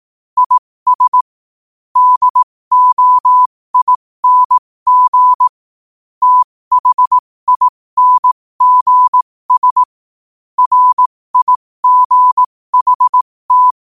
Morse code, heard as a keyed tone, about 18 wpm, spaced as for 15 wpm, 1000 Hz. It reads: IS DOING THINGS RIGHT
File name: qotdfiles / 15wpm